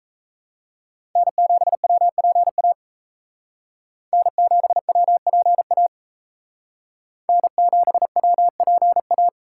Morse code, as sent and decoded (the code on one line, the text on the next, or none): -. --... .-- .--. .- / -. --... .-- .--. .- / -. --... .-- .--. .-
N7WPA N7WPA N7WPA